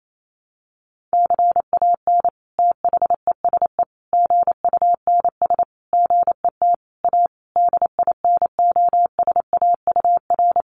Morse code, heard as a keyed tone, about 28 wpm, 700 Hz. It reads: CAN THESE GUNS GET A DINOSAUR
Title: A Sound Of Thunder 28wpm